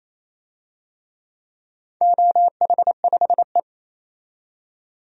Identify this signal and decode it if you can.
Morse code, heard as a keyed tone, about 28 wpm, 700 Hz. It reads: OH5E
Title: DXCC-100 Call Signs - 4 Characters - 28wpm